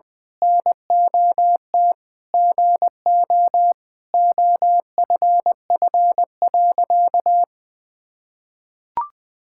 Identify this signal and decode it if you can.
Morse code, heard as a keyed tone, about 21 wpm, 700 Hz. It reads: NOT GO OFF.
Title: Sentences with 4-Word Phrases - Spoken Beginning - 20wpm